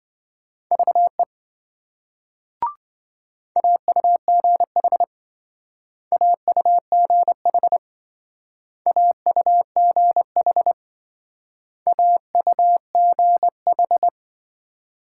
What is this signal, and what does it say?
Morse code, heard as a keyed tone, about 26 wpm, 700 Hz. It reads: VE AUGH AUGH AUGH AUGH